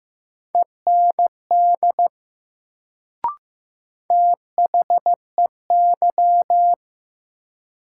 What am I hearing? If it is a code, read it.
Morse code, heard as a keyed tone, about 16 wpm, 700 Hz. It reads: END THEY
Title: Sets of 2 Words - New Word Drill - 15wpm